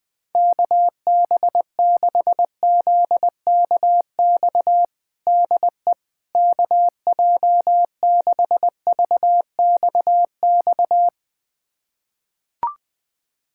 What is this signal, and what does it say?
Morse code, heard as a keyed tone, about 20 wpm, 700 Hz. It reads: KB6ZKX DE KJ6VXX
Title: USA Callsign DE Callsign - 6 Characters - 20wpm